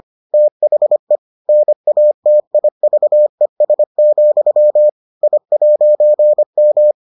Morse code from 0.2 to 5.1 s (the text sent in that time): THE NATIVES,